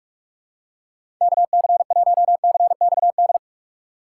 Morse code, heard as a keyed tone, about 43 wpm, 700 Hz. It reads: KC1CXD
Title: USA Call Signs - 6 Characters - 3x Character Spacing - 45wpm